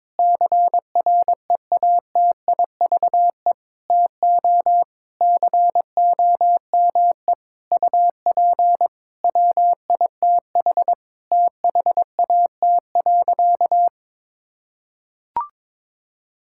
Morse code, heard as a keyed tone, about 22 wpm, 700 Hz. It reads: CREATIVE TO COME UP WITH THAT.
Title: Sentences from Top 1000 Words - Review Part 5 - 22wpm